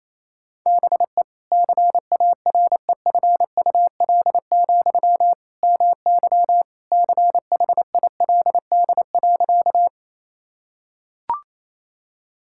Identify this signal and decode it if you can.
Morse code, heard as a keyed tone, about 28 wpm, 700 Hz. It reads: BE CAREFUL, MY CHILD.
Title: Sentences from Top 1600 Words 28wpm